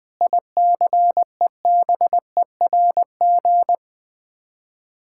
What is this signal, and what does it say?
Morse code, heard as a keyed tone, about 20 wpm, 700 Hz. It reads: ICEBERG
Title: Sets of 8 Words - 20wpm